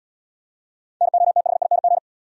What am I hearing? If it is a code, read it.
Morse code, heard as a keyed tone, about 74 wpm, 700 Hz. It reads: NOTHING